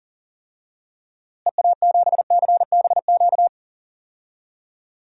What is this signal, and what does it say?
Morse code, heard as a keyed tone, about 37 wpm, 700 Hz. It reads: EA7CBQ
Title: DXCC-100 Call Signs - 6 Characters - Repeated 3x - 40wpm